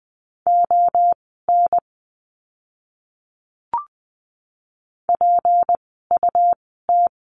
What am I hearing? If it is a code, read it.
Morse code, heard as a keyed tone, about 20 wpm, spaced as for 14 wpm, 700 Hz. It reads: ON PUT
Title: 192 Lesson 31f - Letter v - Words 20wpm